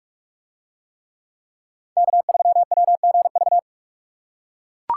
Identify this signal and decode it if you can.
Morse code, heard as a keyed tone, about 43 wpm, 700 Hz. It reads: K3WGV